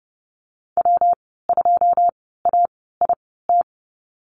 Morse code, heard as a keyed tone, about 30 wpm, spaced as for 17 wpm, 700 Hz. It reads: W2AIT
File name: Lesson 19g - Letter w - Call Signs 30wpm